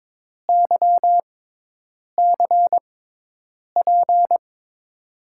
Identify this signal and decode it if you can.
Morse code, heard as a keyed tone, about 22 wpm, spaced as for 8 wpm, 700 Hz. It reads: YCP
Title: USA Call Signs - 6 Characters - 6x Character Spacing - 22wpm